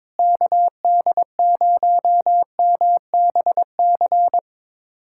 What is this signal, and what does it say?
Morse code, heard as a keyed tone, about 22 wpm, 700 Hz. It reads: KD0MBC
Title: USA Callsign DE Callsign - 6 Characters - 22wpm